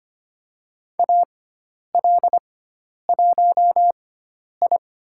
Morse code, heard as a keyed tone, about 25 wpm, spaced as for 10 wpm, 700 Hz. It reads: AL1I
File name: Lesson 16g - Letter c - Call Signs 25wpm